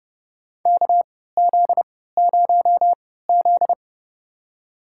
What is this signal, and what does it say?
Morse code, heard as a keyed tone, about 31 wpm, spaced as for 17 wpm, 700 Hz. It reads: KZ0Z
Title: USA Call Signs - 4 Characters - 3x Character Spacing - 30wpm